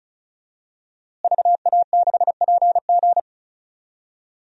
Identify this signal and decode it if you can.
Morse code, heard as a keyed tone, about 36 wpm, 700 Hz. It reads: VA6PG
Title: DXCC-100 Call Signs - 5 Characters - Repeated 2x - 35wpm